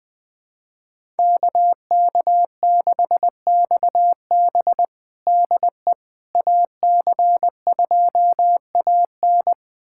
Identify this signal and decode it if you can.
Morse code, heard as a keyed tone, about 20 wpm, 700 Hz. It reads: KK6XB DE AC2AN
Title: USA Callsign DE Callsign - 5 Characters - 20wpm